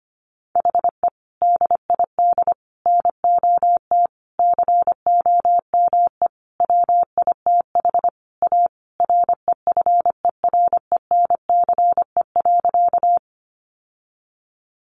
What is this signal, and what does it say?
Morse code, heard as a keyed tone, about 25 wpm, 700 Hz. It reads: HE DID NOT COME WITH A REFERENCE.
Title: Sentences from Top 1200 Words 25wpm